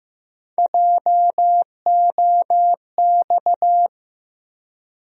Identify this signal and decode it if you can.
Morse code, heard as a keyed tone, about 15 wpm, 700 Hz. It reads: JOX
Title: USA Call Sign Postfixes - 3 Letters 15wpm